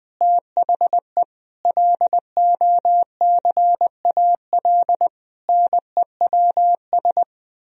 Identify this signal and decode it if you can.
Morse code, heard as a keyed tone, about 20 wpm, 700 Hz. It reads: THE LOCAL NEWS
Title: Sentences from Top 1000 Words - Review Part 3 - 20wpm